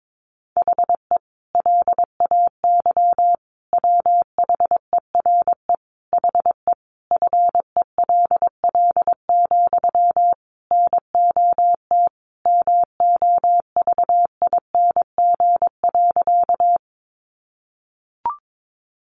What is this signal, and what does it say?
Morse code, heard as a keyed tone, about 22 wpm, 700 Hz. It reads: HE LAY WHERE HE FELL, NOT MOVING.
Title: A Sound Of Thunder 22wpm